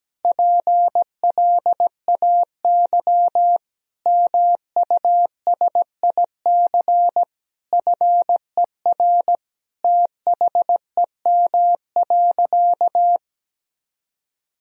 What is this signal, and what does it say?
Morse code, heard as a keyed tone, about 17 wpm, 700 Hz. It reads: PLAY MUSIC FER THEM.
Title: Sentences from Top 300 Words and Abbreviations 17wpm